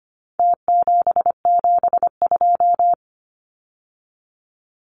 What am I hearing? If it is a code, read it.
Morse code, heard as a keyed tone, about 25 wpm, 700 Hz. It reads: T772
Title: Numbers - Contest Style - 4-Digit 25wpm